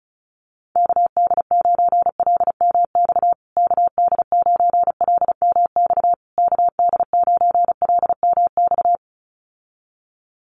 Morse code, heard as a keyed tone, about 34 wpm, 700 Hz. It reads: KD9LMX KD9LMX KD9LMX
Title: USA Call Signs Repeated 3x 35wpm